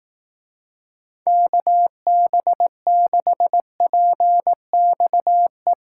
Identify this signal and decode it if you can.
Morse code, heard as a keyed tone, about 18 wpm, 700 Hz. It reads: KB6PXE
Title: Speed Racing - USA Call Signs 15wpm